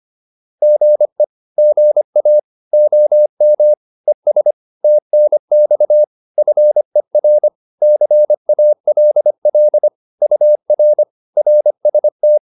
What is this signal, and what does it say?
Morse code, heard as a keyed tone, about 25 wpm, 600 Hz. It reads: GE GA OM ES TNX FER CALL UR RST